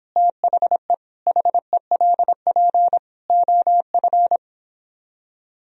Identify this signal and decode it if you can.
Morse code, heard as a keyed tone, about 26 wpm, 700 Hz. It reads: THE HELP OF